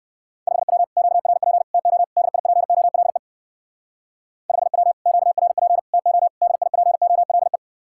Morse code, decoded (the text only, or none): 5W QRP TO DIPOLE 5W QRP TO DIPOLE